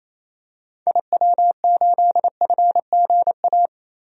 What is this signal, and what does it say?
Morse code, heard as a keyed tone, about 28 wpm, 700 Hz. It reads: IW8FGA
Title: DXCC-100 Call Signs - 6 Characters - Repeated 3x - 28wpm